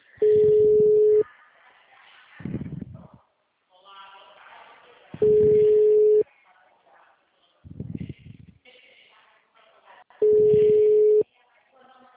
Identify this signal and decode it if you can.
Ringback tone (Europe, 425 Hz), with speech.
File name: out- -2011- - - .85658